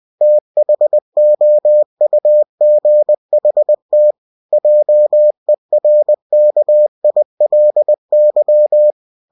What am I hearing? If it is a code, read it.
Morse code, heard as a keyed tone, about 20 wpm, 600 Hz. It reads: THOUGHT JERKILY
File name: cw-books / winnie-the-pooh / winnie-20wpm-page-0070